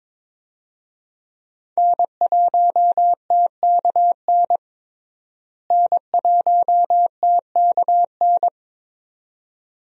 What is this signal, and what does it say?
Morse code, heard as a keyed tone, about 22 wpm, 700 Hz. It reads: N1TKN N1TKN
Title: USA Call Signs - 5 Characters - Repeated 2x - 22wpm